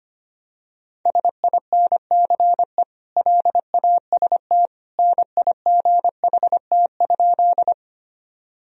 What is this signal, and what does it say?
Morse code, heard as a keyed tone, about 25 wpm, 700 Hz. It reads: SINCE LAST NIGHT?
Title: Sentences from Top 1000 Words - Review Part 1 - 25wpm